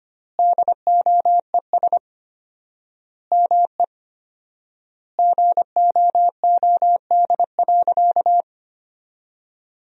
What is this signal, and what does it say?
Morse code, heard as a keyed tone, about 25 wpm, 700 Hz. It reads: DOES ME GOOD.
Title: 4x Word-Spacing - Sentences from Top 200 Words - 25wpm